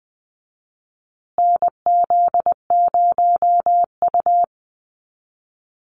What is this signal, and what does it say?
Morse code, heard as a keyed tone, about 20 wpm, 700 Hz. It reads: NZ0U
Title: USA Call Signs Repeated 3x 20wpm